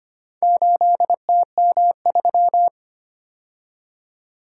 Morse code, heard as a keyed tone, about 25 wpm, 700 Hz. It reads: 8TM3
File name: practice / 25wpm / Four Letter-Number Combinations - 25wpm